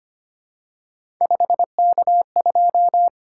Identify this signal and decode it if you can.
Morse code, heard as a keyed tone, about 25 wpm, 700 Hz. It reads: 5K2
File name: Three Letter-Number Combinations - 25wpm